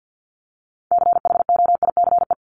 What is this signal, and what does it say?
Morse code, heard as a keyed tone, about 50 wpm, 700 Hz. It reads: CHOICE
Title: Top 751-1000 Words 50wpm